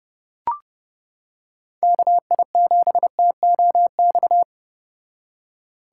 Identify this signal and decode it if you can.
Morse code, heard as a keyed tone, about 29 wpm, 700 Hz. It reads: KI7TOX